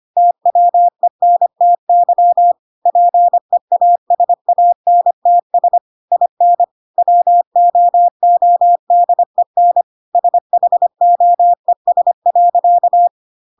Morse code, heard as a keyed tone, about 25 wpm, 700 Hz. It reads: TWENTY PEASANTS IN WOODEN SHOES.
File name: Pinocchio / Pinocchio ch3 25 25 ik1hge